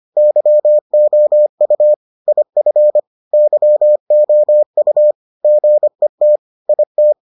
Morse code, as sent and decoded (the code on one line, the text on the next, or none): -.-- --- ..- / .. ..-. / -.-- --- ..- / --. - / .. -
YOU IF YOU GET IT